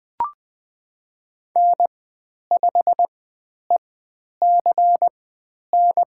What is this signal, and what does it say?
Morse code, heard as a keyed tone, about 21 wpm, spaced as for 10 wpm, 700 Hz. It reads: N5ECN
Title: 109 Lesson 18g - Letter m - Call Signs 20wpm